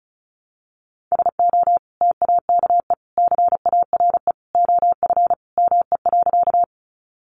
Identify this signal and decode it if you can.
Morse code, heard as a keyed tone, about 34 wpm, 700 Hz. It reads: SO TAKE CARE OF ME.
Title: Sentences from Top 300 Words 35wpm